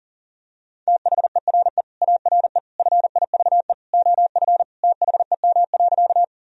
Morse code, heard as a keyed tone, about 40 wpm, 700 Hz. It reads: THERE ARE FIVE OF THEM.
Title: Sentences from Top 1000 Words - Review Part 2 - 40wpm